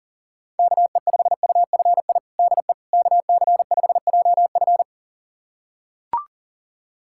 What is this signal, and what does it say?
Morse code, heard as a keyed tone, about 40 wpm, 700 Hz. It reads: KE5UFI DE KC5JF
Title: USA Callsign DE Callsign 40wpm